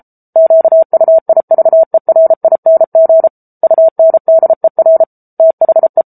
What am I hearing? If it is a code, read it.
Morse code, heard as a keyed tone, about 34 wpm, 650 Hz. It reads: QUIVERING UNDER THE
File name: Book Two / 07-Chapter 7 - 33 WPM